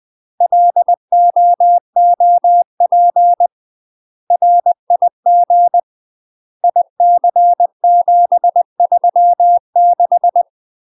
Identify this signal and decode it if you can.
Morse code, heard as a keyed tone, about 20 wpm, 700 Hz. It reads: LOOP RIG IC736
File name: UK MASTER QSOs TRAINING / 20WPM / UK MASTER QSOs 0055